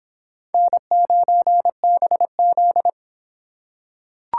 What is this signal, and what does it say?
Morse code, heard as a keyed tone, about 24 wpm, 700 Hz. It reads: N9BZ